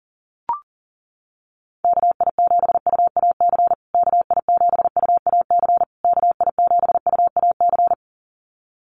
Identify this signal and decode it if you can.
Morse code, heard as a keyed tone, about 40 wpm, 700 Hz. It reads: KI7UAC KI7UAC KI7UAC
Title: USA Call Signs Repeated 3x 40wpm